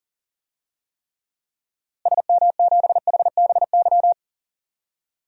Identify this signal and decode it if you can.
Morse code, heard as a keyed tone, about 40 wpm, 700 Hz. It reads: SM7HBY
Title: DXCC-100 Call Signs - 6 Characters - Repeated 3x - 40wpm